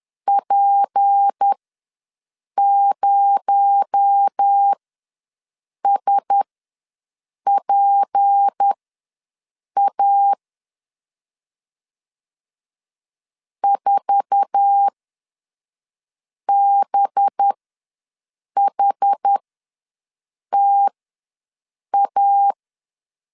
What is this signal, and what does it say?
Morse code, heard as a keyed tone, about 10 wpm, spaced as for 6 wpm, 800 Hz. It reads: P0SPA 4BHTA
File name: LEZIONI / 10 SIGNS / 10D COMA ws 10wpm 4314